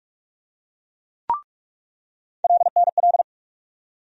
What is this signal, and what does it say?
Morse code, heard as a keyed tone, about 46 wpm, 700 Hz. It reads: LNL
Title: USA Call Sign Postfixes - 3 Letters 45wpm